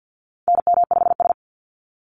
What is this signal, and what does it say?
Morse code, heard as a keyed tone, about 48 wpm, 700 Hz. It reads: NM5S
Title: NA SOTA Activator Callsigns 50wpm